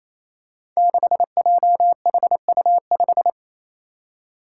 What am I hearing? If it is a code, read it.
Morse code, heard as a keyed tone, about 29 wpm, 700 Hz. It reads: 6JHU5
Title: Five Challenging Letter-Number Combinations 28wpm